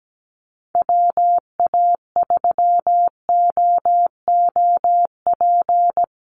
Morse code, heard as a keyed tone, about 17 wpm, 700 Hz. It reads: WA3OOP
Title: USA Call Signs Repeated 3x 17wpm